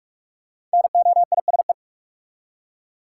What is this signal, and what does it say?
Morse code, heard as a keyed tone, about 45 wpm, 700 Hz. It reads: NOISE
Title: Five Letter Words - 45wpm